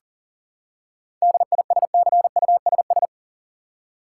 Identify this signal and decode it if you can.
Morse code, heard as a keyed tone, about 40 wpm, 700 Hz. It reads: DISCUSS
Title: Seven Letter Words - 40wpm